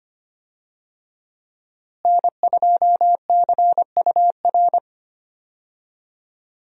Morse code, heard as a keyed tone, about 25 wpm, 700 Hz. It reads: N2CUR
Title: USA Call Signs - 5 Characters - Repeated 3x - 25wpm